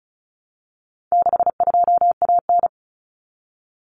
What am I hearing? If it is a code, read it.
Morse code, heard as a keyed tone, about 34 wpm, 700 Hz. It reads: 62AN
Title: Common Cut Numbers - 4-Digit 35wpm